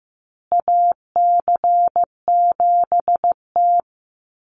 Morse code, heard as a keyed tone, about 15 wpm, 700 Hz. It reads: AC7T
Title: Call Signs Repeated 2x - 15wpm